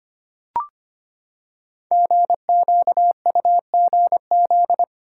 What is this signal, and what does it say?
Morse code, heard as a keyed tone, about 25 wpm, 700 Hz. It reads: GQUGZ
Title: Code Groups - 5 Letters - 25wpm